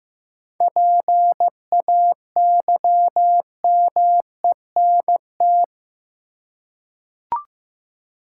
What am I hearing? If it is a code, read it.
Morse code, heard as a keyed tone, about 15 wpm, 700 Hz. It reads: PAYMENT
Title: Seven Letter Words - 15wpm